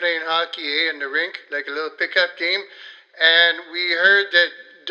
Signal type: Emergency Alert System (SAME) tones